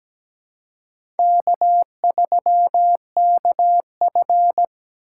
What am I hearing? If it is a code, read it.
Morse code, heard as a keyed tone, about 17 wpm, 700 Hz. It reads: K3KF
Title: USA Call Signs - 6x Character Spacing - 17wpm